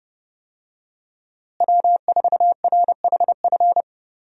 Morse code, heard as a keyed tone, about 30 wpm, 700 Hz. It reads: W4RHF
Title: USA Call Signs - 5 Characters - Repeated 3x - 30wpm